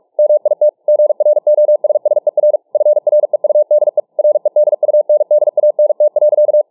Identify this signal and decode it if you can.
Morse code, heard as a keyed tone, about 43 wpm, 600 Hz. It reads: MIT GROSSER FREUDE REDUNDANT.